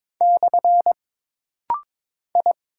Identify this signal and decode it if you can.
Morse code, heard as a keyed tone, about 22 wpm, 700 Hz. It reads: / I
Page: This rapid-fire timing is geared to push your ICR (Instant Character Recognition) skills.